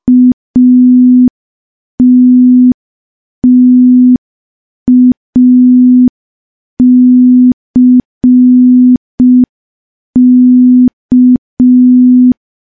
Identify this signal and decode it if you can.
Morse code, heard as a keyed tone, about 5 wpm, 250 Hz. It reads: ATTACK
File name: morse / morse-training-05WPM